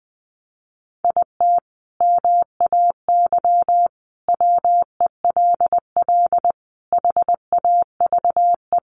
Morse code, heard as a keyed tone, about 20 wpm, 700 Hz. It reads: IT MAY WELL HAVE
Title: Sentences from Top 1000 Words 20wpm